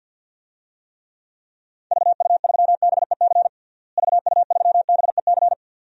Morse code, heard as a keyed tone, about 48 wpm, 700 Hz. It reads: VU3BEC VU3BEC
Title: DXCC-100 Call Signs - 6 Characters - Repeated 2x - 50wpm